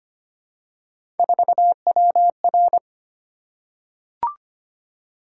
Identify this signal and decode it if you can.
Morse code, heard as a keyed tone, about 25 wpm, 700 Hz. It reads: 4WR
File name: Three Letter-Number Combinations - 25wpm